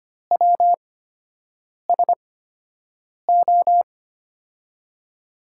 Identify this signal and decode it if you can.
Morse code, heard as a keyed tone, about 25 wpm, spaced as for 7 wpm, 700 Hz. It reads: WSO
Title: USA Call Signs - 5 Characters - 8x Character Spacing - 25wpm